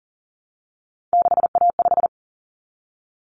Morse code, heard as a keyed tone, about 40 wpm, 700 Hz. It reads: 6A5
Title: Common Cut Numbers - 3-Digit 40wpm